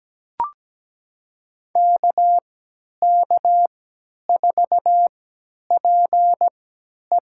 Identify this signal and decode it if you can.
Morse code, heard as a keyed tone, about 17 wpm, spaced as for 10 wpm, 700 Hz. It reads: KK4PE
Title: USA Call Signs - 3x Character Spacing - 17wpm